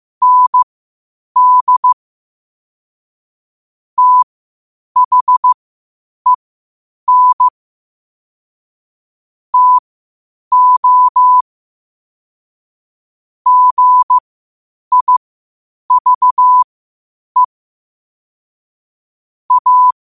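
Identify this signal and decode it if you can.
Morse code, heard as a keyed tone, about 15 wpm, spaced as for 8 wpm, 1000 Hz. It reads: ND THEN TO GIVE A
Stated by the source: Morse code quotes at 8 WPM.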